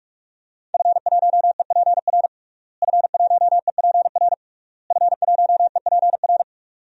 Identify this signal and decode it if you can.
Morse code, heard as a keyed tone, about 46 wpm, 700 Hz. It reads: F1EPR F1EPR F1EPR
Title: DXCC-100 Call Signs - 5 Characters - Repeated 3x - 45wpm